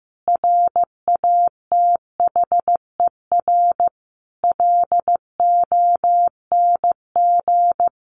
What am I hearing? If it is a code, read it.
Morse code, heard as a keyed tone, about 15 wpm, 700 Hz. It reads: RATHER LONG
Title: Sentences from Top 700 Words 15wpm